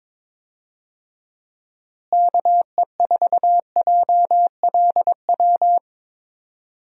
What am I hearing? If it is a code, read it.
Morse code, heard as a keyed tone, about 22 wpm, 700 Hz. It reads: KE4JLW
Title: USA Call Signs - 6 Characters - Repeated 2x - 22wpm